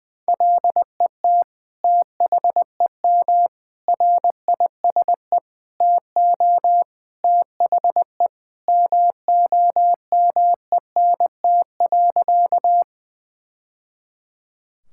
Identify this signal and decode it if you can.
Morse code, heard as a keyed tone, about 20 wpm, 700 Hz. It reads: LET THEM RISE TO THE MOMENT.
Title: Sentences from Top 1000 Words - Review Part 6 - 20wpm